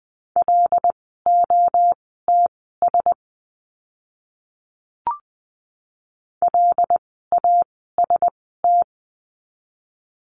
Morse code, heard as a keyed tone, about 20 wpm, spaced as for 15 wpm, 700 Hz. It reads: LOTS LAST
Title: 075 Lesson 13f - Letter L - Words 20wpm